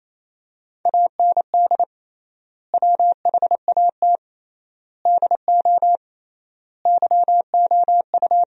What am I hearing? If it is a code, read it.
Morse code, heard as a keyed tone, about 28 wpm, 700 Hz. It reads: AND WHAT DO YOU
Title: 3x Word-Spacing - Sentences from Top 100 Words - 28wpm